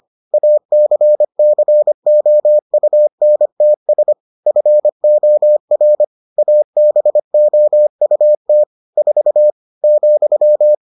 Morse code, heard as a keyed tone, about 25 wpm, 600 Hz. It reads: ACCOUNTS FOR ABOUT 4 ,